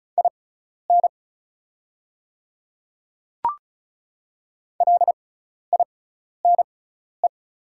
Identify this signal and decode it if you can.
Morse code, heard as a keyed tone, about 34 wpm, spaced as for 12 wpm, 700 Hz. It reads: IN LINE
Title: 6x Character Spacing - Top 200 Words - 35wpm